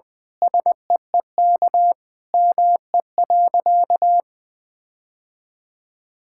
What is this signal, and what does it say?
Morse code, heard as a keyed tone, about 21 wpm, 700 Hz. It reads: SEEK ME.
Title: Sentences from Top 1600 Words 20wpm